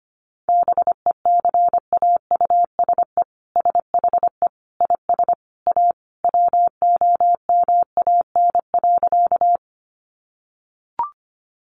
Morse code, heard as a keyed tone, about 25 wpm, 700 Hz. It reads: BECAUSE SHE IS A WOMAN.
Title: Sentences from Top 1100 Words 25wpm